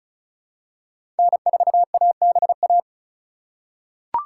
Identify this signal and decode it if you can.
Morse code, heard as a keyed tone, about 34 wpm, 700 Hz. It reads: N4ABA
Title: USA Call Signs - 5 Characters - 2x Character Spacing - 35wpm